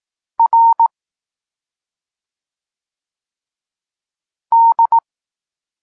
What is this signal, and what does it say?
Morse code, heard as a keyed tone, about 18 wpm, 920 Hz. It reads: R D